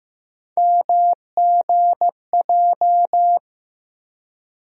Morse code, heard as a keyed tone, about 15 wpm, 700 Hz. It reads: MGJ
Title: USA Call Sign Postfixes - 3 Letters 15wpm